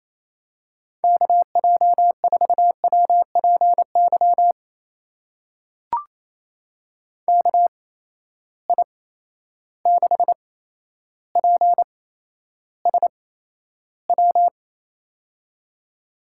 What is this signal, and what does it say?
Morse code, heard as a keyed tone, about 28 wpm, spaced as for 12 wpm, 700 Hz. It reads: KJ4WPY KI6PSW
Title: USA Call Signs - 6 Characters - 8x Character Spacing - 28wpm